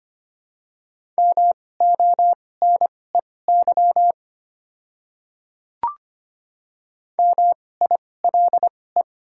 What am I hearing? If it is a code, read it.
Morse code, heard as a keyed tone, about 25 wpm, spaced as for 18 wpm, 700 Hz. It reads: MONEY MILE
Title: 2x Character Spacing - Top 500 Words - 25wpm